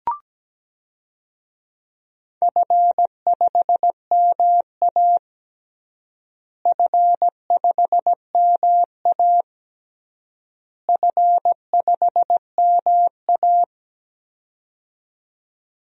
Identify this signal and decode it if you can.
Morse code, heard as a keyed tone, about 17 wpm, 700 Hz. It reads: F5MA F5MA F5MA
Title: DXCC-100 Call Signs - 4 Characters - Repeated 3x - 17wpm